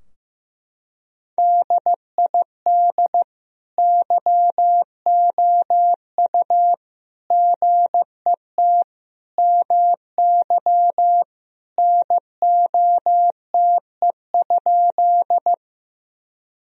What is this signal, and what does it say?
Morse code, heard as a keyed tone, about 15 wpm, 700 Hz. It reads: DID YOU GET MY NOTE?
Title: Sentences from Top 1000 Words - Review Part 2 - 15wpm